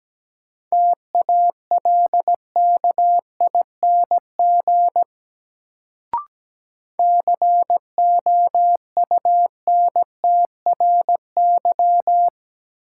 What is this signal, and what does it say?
Morse code, heard as a keyed tone, about 17 wpm, 700 Hz. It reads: TALKING COUNTRY
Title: Sets of 2 Words - New Word Drill - 17wpm